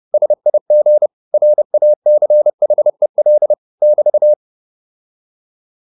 Morse code, heard as a keyed tone, about 30 wpm, 600 Hz. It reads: SIG RACHEL <BT>
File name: QSO301 30